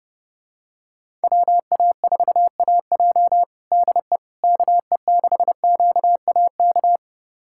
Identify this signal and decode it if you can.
Morse code, heard as a keyed tone, about 30 wpm, 690 Hz. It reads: WA4AJ DE KE6QAK
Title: USA Callsign DE Callsign 30wpm